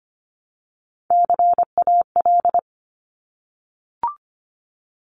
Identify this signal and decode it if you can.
Morse code, heard as a keyed tone, about 25 wpm, 700 Hz. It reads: CAL
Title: Top Three Letter Combinations in Words 25wpm